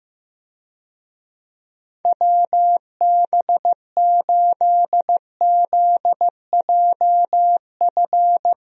Morse code, heard as a keyed tone, about 16 wpm, 700 Hz. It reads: WB8ZJF
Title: USA Call Signs - 6 Characters - Repeated 3x - 15wpm